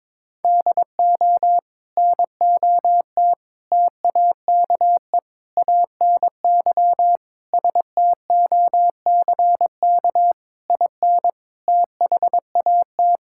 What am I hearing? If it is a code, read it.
Morse code, heard as a keyed tone, about 22 wpm, 700 Hz. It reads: DO NOT TAKE ANY STOCK IN THAT
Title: Sentences from Top 1000 Words - Review Part 5 - 22wpm